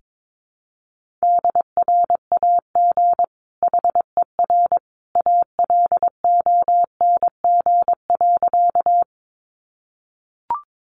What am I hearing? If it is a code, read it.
Morse code, heard as a keyed tone, about 22 wpm, 700 Hz. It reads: DRAG HER ALONG.
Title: Sentences from Top 1500 Words 22wpm